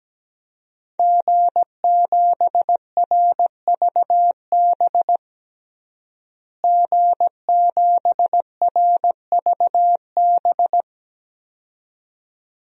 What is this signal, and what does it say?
Morse code, heard as a keyed tone, about 17 wpm, 700 Hz. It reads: G7RVB G7RVB
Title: DXCC-100 Call Signs - 5 Characters - Repeated 2x - 17wpm